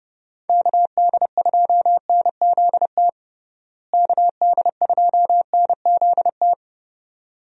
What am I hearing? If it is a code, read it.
Morse code, heard as a keyed tone, about 30 wpm, 700 Hz. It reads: KD2NZT KD2NZT
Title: USA Call Signs - 6 Characters - Repeated 2x - 30wpm